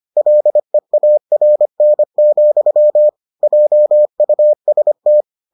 Morse code, heard as a keyed tone, about 25 wpm, 600 Hz. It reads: LEARN, JUST